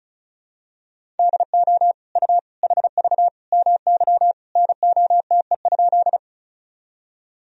Morse code, heard as a keyed tone, about 36 wpm, 700 Hz. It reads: DO U HV MY NOTE?
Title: Sentences from Top 500 Words and Abbreviations 35wpm